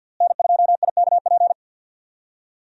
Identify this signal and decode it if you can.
Morse code, heard as a keyed tone, about 51 wpm, 700 Hz. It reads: N2IKP
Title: USA Call Signs - 5 Characters - Repeated 3x - 50wpm